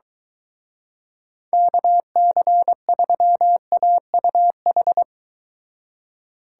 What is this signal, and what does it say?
Morse code, heard as a keyed tone, about 24 wpm, 700 Hz. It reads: KC3AUH